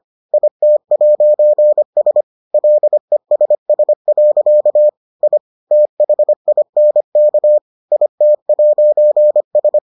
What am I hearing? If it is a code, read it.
Morse code, heard as a keyed tone, about 25 wpm, 600 Hz. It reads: IT'S LESS. I THINK IT'S